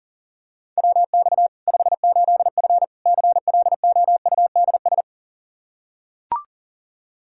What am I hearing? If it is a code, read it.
Morse code, heard as a keyed tone, about 40 wpm, 700 Hz. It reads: WX 58F CLOUDS
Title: Practice QSO - Exchange 2 - Info - 40wpm